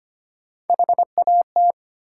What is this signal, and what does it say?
Morse code, heard as a keyed tone, about 25 wpm, 700 Hz. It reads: HAT
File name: practice / 25wpm / Easy Words with Hints - 25wpm